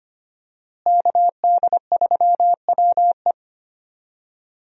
Morse code, heard as a keyed tone, about 25 wpm, 700 Hz. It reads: KD3WE
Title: USA Call Signs - 5 Characters - Repeated 3x - 25wpm